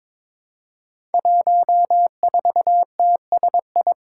Morse code, heard as a keyed tone, about 23 wpm, 700 Hz. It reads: 14TSI
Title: Five Challenging Letter-Number Combinations 22wpm